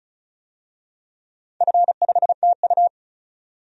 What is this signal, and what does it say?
Morse code, heard as a keyed tone, about 36 wpm, 690 Hz. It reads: F5TU
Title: DXCC-100 Call Signs - 4 Characters - 35wpm